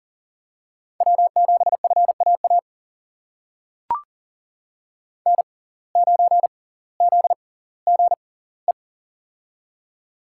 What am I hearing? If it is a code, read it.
Morse code, heard as a keyed tone, about 39 wpm, spaced as for 20 wpm, 700 Hz. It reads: W7FAA N9ZGE
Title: USA Call Signs - 5 Characters - 6x Character Spacing - 40wpm